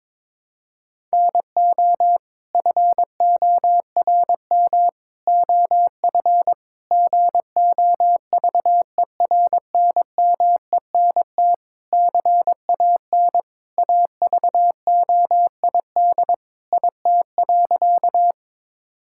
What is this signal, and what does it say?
Morse code, heard as a keyed tone, about 22 wpm, 700 Hz. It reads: NO FORM OF GOVERNMENT CAN AVOID IT.
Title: Sentences from Top 1000 Words - Review Part 6 - 22wpm